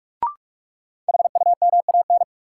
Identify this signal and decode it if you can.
Morse code, heard as a keyed tone, about 46 wpm, 700 Hz. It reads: HUMAN
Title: Sentences from Top 600 Words - New Word Drill - 45wpm